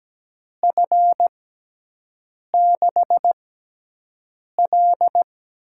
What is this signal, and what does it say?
Morse code, heard as a keyed tone, about 17 wpm, 700 Hz. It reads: F 6 L
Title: USA Call Signs - 6 Characters - 6x Character Spacing - 17wpm